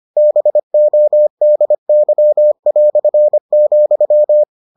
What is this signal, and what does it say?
Morse code, heard as a keyed tone, about 25 wpm, 600 Hz. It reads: BODY",